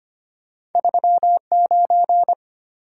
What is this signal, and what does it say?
Morse code, heard as a keyed tone, about 33 wpm, 700 Hz. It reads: 39